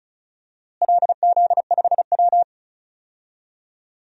Morse code, heard as a keyed tone, about 34 wpm, 700 Hz. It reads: LZ5W
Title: DXCC-100 Call Signs - 4 Characters - Repeated 2x - 35wpm